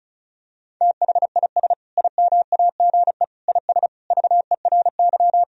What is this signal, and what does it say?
Morse code, heard as a keyed tone, about 36 wpm, 700 Hz. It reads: THIS IMAGE IS VERY
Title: Sentences from Top 1000 Words - Review Part 4 - 35wpm